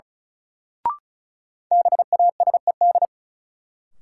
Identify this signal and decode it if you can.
Morse code, heard as a keyed tone, about 36 wpm, 700 Hz. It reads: BASED
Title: Sets of 2 Words - New Word Drill - 35wpm